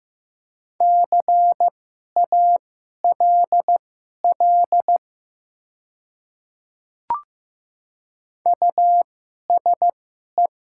Morse code, heard as a keyed tone, about 15 wpm, spaced as for 11 wpm, 700 Hz. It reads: CALL USE
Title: 2x Character Spacing - Top 100 Words - 15wpm